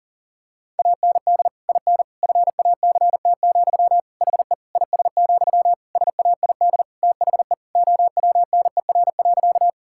Morse code, heard as a keyed tone, about 40 wpm, 700 Hz. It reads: AND IN FACT, HE IS, SAID THE OWNER.
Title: Sentences from Top 1600 Words 40wpm